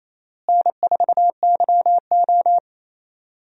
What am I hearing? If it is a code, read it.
Morse code, heard as a keyed tone, about 28 wpm, 700 Hz. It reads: N4YO
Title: USA Call Signs - 4 Characters - Repeated 3x - 28wpm